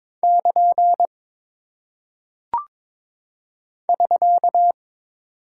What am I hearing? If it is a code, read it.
Morse code, heard as a keyed tone, about 22 wpm, 700 Hz. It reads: <KN> <SK>
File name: practice / 22wpm / Single Punctuation-Prosign - 22wpm